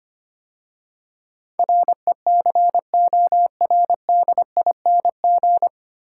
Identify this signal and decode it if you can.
Morse code, heard as a keyed tone, about 25 wpm, 700 Hz. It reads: RECORDING
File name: practice / 25wpm / 9 Letter Words - First 3 Letters Given - 25wpm